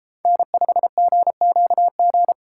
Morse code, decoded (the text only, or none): N5GQG